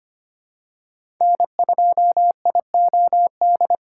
Morse code, heard as a keyed tone, about 25 wpm, 700 Hz. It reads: N2IOD
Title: USA Call Signs - 5 Characters - 2x Character Spacing - 25wpm